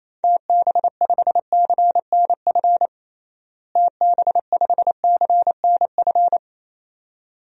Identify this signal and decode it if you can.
Morse code, heard as a keyed tone, about 28 wpm, 700 Hz. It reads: TB5CNF TB5CNF
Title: DXCC-100 Call Signs - 6 Characters - Repeated 2x - 28wpm